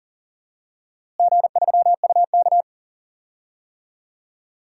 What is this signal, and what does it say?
Morse code, heard as a keyed tone, about 40 wpm, 700 Hz. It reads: G3UK
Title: DXCC-100 Call Signs - 4 Characters - Repeated 3x - 40wpm